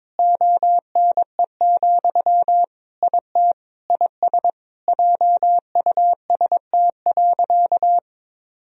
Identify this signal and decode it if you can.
Morse code, heard as a keyed tone, about 22 wpm, 700 Hz. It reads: ONE, IT IS JUST.
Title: Sentences from Top 1000 Words - Review Part 1 - 22wpm